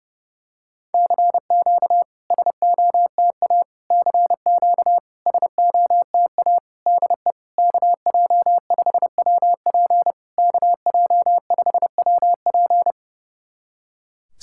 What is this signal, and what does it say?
Morse code, heard as a keyed tone, about 30 wpm, 700 Hz. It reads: CQ SOTA CQ SOTA DE KJ5WP KJ5WP
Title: SOTA QSO Practice 30wpm